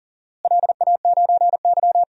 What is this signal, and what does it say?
Morse code, heard as a keyed tone, about 39 wpm, 700 Hz. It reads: LA9Y
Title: DXCC-100 Call Signs - 4 Characters - Repeated 2x - 40wpm